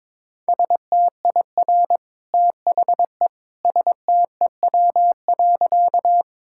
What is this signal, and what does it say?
Morse code, heard as a keyed tone, about 22 wpm, 700 Hz. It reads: STIR THE STEW.
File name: practice / 22wpm / Easy Sentences with Hints - 22wpm